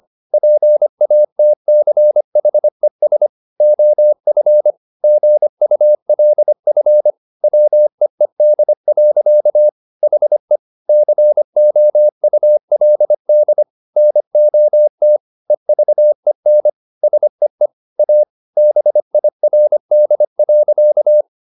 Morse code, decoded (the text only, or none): PATCHES OF GULF WEED. HE COULD NOT EVEN SEE A BIRD.